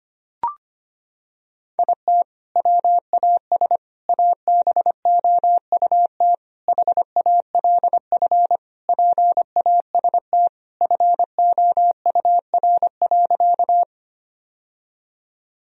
Morse code, heard as a keyed tone, about 25 wpm, 700 Hz. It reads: IT WAS ABOUT HALF PAST FOUR.
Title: Sentences from Top 1000 Words - Review Part 3 - 25wpm